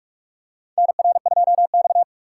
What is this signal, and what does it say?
Morse code, heard as a keyed tone, about 46 wpm, 690 Hz. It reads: NR2X
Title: USA Call Signs - 4 Characters - 3x Character Spacing - 45wpm